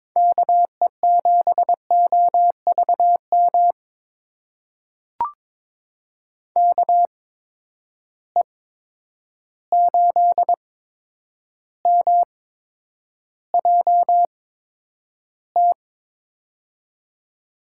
Morse code, heard as a keyed tone, about 22 wpm, spaced as for 9 wpm, 700 Hz. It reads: KE7OVM KE8MJT
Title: USA Call Signs - 6 Characters - 8x Character Spacing - 22wpm